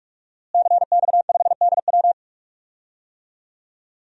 Morse code, heard as a keyed tone, about 44 wpm, 700 Hz. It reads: CX5DW
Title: DXCC-100 Call Signs - 5 Characters - Repeated 3x - 45wpm